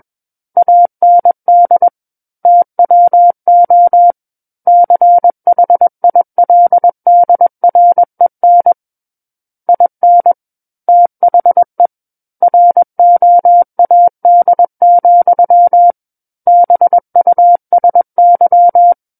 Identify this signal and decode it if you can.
Morse code, heard as a keyed tone, about 21 wpm, 700 Hz. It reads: AND TWO CHILDREN IN THE ROAD, BUSY
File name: Book One / 12-Chapter 12 - 21 WPM